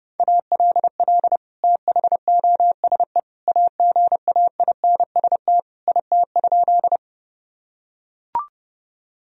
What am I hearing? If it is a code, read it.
Morse code, heard as a keyed tone, about 32 wpm, 700 Hz. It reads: ALL THOSE AGAINST IT?
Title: Sentences from Top 1000 Words - Review Part 1 - 30wpm